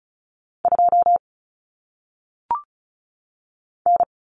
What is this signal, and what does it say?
Morse code, heard as a keyed tone, about 36 wpm, 700 Hz. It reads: 2 N
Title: Common Cut Numbers - 1-Digit 35wpm